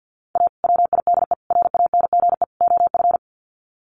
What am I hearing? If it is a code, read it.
Morse code, heard as a keyed tone, about 50 wpm, 700 Hz. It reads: A WIDE RANGE OF
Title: Sets of 4 Words 50wpm